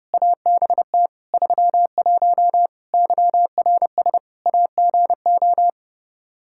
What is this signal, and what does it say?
Morse code, heard as a keyed tone, about 32 wpm, 700 Hz. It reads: ABT 31 YRS AGO
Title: Practice QSO - Exchange 3 - More Info - 30wpm